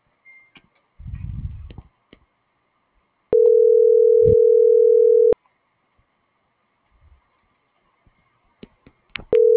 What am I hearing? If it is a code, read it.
Ringback tone (North America), 440 + 480 Hz.